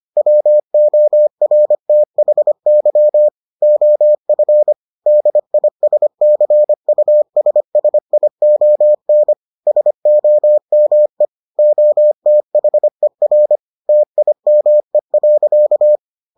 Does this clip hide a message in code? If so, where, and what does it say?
Yes: Morse code from 0.2 to 16.0 s, heard as a keyed tone, about 25 wpm, 600 Hz. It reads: WORTHY OF DISCUSSION SOME OTHER TIME.